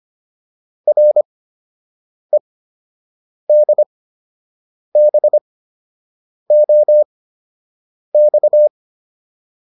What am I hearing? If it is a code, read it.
Morse code, heard as a keyed tone, about 25 wpm, spaced as for 7 wpm, 600 Hz. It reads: REDBOX